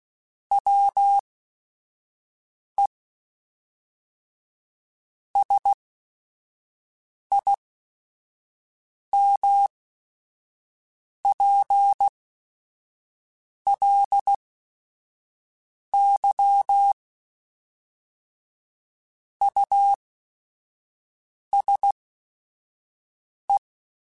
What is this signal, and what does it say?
Morse code, heard as a keyed tone, about 16 wpm, spaced as for 5 wpm, 800 Hz. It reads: WE SIMPLY USE
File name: Technical Subjects / Technical Subjects-mp3s / Sw Pwr Sup @5